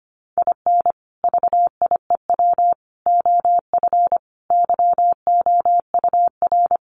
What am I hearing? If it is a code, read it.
Morse code, heard as a keyed tone, about 25 wpm, 700 Hz. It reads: IN VIEW OF YOUR
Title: Sentences from Top 900 Words 25wpm